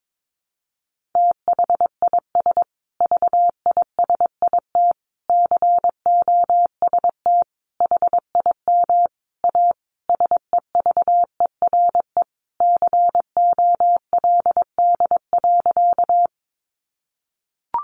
Morse code, heard as a keyed tone, about 22 wpm, 700 Hz. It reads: THIS VISIT COST HIM A SEVERE COLD.
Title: Sentences from Top 1400 Words 22wpm